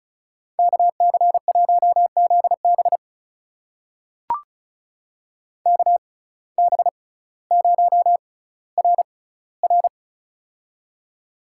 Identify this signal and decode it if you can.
Morse code, heard as a keyed tone, about 36 wpm, spaced as for 18 wpm, 700 Hz. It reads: KC1ZB KB0RR
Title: USA Call Signs - 5 Characters - 6x Character Spacing - 35wpm